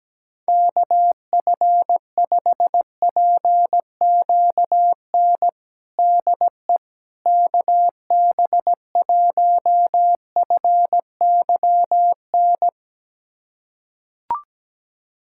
Morse code, heard as a keyed tone, about 17 wpm, 700 Hz. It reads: KF5PQN DE KB1FYN
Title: USA Callsign DE Callsign - 6 Characters - 17wpm